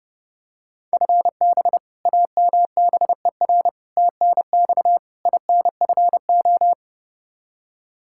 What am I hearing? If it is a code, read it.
Morse code, heard as a keyed tone, about 30 wpm, 700 Hz. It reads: FB AMBER TNX INFO
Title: Practice QSO - Exchange 3 - More Info - 30wpm